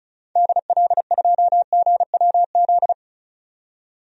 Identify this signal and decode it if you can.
Morse code, heard as a keyed tone, about 34 wpm, 700 Hz. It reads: DL2GWZ
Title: DXCC-100 Call Signs - 6 Characters - 35wpm